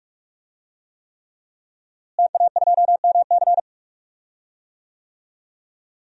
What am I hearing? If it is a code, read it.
Morse code, heard as a keyed tone, about 46 wpm, 700 Hz. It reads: TA2MC
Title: DXCC-100 Call Signs - 5 Characters - Repeated 3x - 45wpm